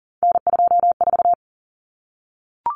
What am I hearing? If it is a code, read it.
Morse code, heard as a keyed tone, about 39 wpm, 700 Hz. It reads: N24
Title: Common Cut Numbers - 3-Digit 40wpm